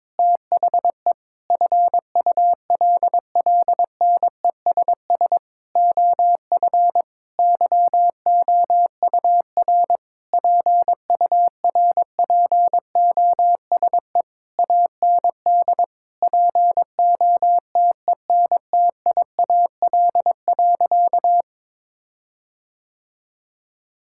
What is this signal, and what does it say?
Morse code, heard as a keyed tone, about 22 wpm, 700 Hz. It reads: THE FULLNESS OF YOUR PURPOSE AND POTENTIAL.
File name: Sentences with 9-Word Phrases - 22wpm